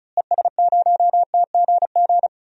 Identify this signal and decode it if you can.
Morse code, heard as a keyed tone, about 36 wpm, 700 Hz. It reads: ES0TGG
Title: DXCC-100 Call Signs - 6 Characters - Repeated 3x - 35wpm